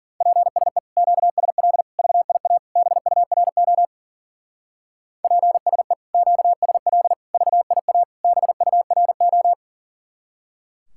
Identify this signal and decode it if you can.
Morse code, heard as a keyed tone, about 50 wpm, 700 Hz. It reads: PSE QSL VIA BURO PSE QSL VIA BURO